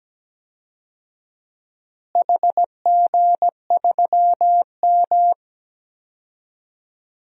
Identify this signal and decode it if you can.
Morse code, heard as a keyed tone, about 17 wpm, 700 Hz. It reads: HG3M
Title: DXCC-100 Call Signs - 4 Characters - Repeated 3x - 17wpm